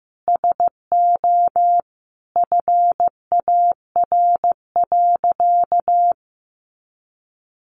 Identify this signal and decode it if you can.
Morse code, heard as a keyed tone, about 15 wpm, 700 Hz. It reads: SO FAR.
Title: Sentences from Top 800 Words 15wpm